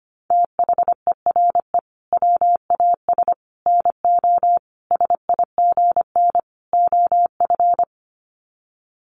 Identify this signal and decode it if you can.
Morse code, heard as a keyed tone, about 25 wpm, 700 Hz. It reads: THERE WAS NO SIGN OF
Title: Sets of 5 Words Encore2 25wpm